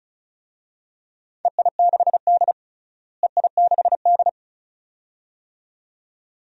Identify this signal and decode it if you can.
Morse code, heard as a keyed tone, about 36 wpm, 700 Hz. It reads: EI6D EI6D
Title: DXCC-100 Call Signs - 4 Characters - Repeated 2x - 35wpm